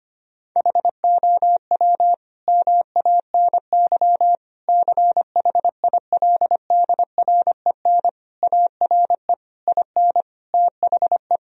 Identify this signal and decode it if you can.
Morse code, heard as a keyed tone, about 25 wpm, 700 Hz. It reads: HOW MANY CHILDREN ARE IN THE
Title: Sentences from Top 1000 Words - Review Part 4 - 25wpm